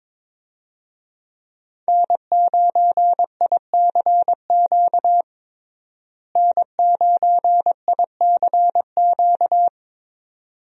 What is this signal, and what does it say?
Morse code, heard as a keyed tone, about 22 wpm, 700 Hz. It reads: N9ICQ N9ICQ
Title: USA Call Signs - 5 Characters - Repeated 2x - 22wpm